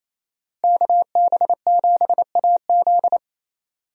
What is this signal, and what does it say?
Morse code, heard as a keyed tone, about 28 wpm, 700 Hz. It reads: KB7AZ
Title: USA Call Signs - 5 Characters - Repeated 3x - 28wpm